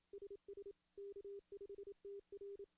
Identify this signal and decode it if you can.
Morse code, heard as a keyed tone, about 27 wpm, 410 Hz. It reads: SS K5TR
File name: CW / CW-SS-k5tr- -1700